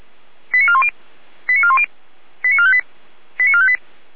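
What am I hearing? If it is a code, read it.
Five-tone selective call (ZVEI), heard as quick runs of five tones, tones of about 70 ms. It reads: 89410 89410 89458 89458
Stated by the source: ZVEI five-tone selective call 89230 received by SDR, fire brigade alerting (feuerwehr-orlamuende).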